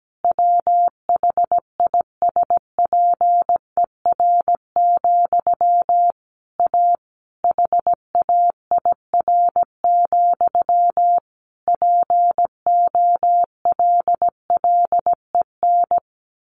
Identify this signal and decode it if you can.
Morse code, heard as a keyed tone, about 17 wpm, 700 Hz. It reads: WHISPER, A HAIR, POLLEN
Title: A Sound Of Thunder 17wpm